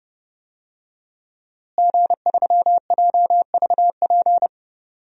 Morse code, heard as a keyed tone, about 30 wpm, 700 Hz. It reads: G3JVP
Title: DXCC-100 Call Signs - 5 Characters - Repeated 2x - 30wpm